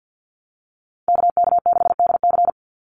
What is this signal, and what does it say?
Morse code, heard as a keyed tone, about 50 wpm, 700 Hz. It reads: KK6DC
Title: USA Call Signs 50wpm